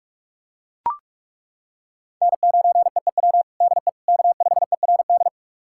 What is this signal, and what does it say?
Morse code, heard as a keyed tone, about 41 wpm, 700 Hz. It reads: N9EEW DE K5ERD
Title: USA Callsign DE Callsign - 5 Characters - 45wpm